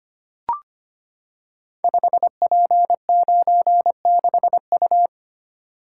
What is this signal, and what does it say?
Morse code, heard as a keyed tone, about 25 wpm, 700 Hz. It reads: 5P96U
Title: Code Groups - 5 Letter:Numbers - 25wpm